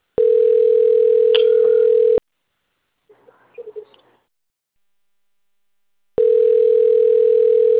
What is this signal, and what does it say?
Ringback tone (North America), 440 + 480 Hz.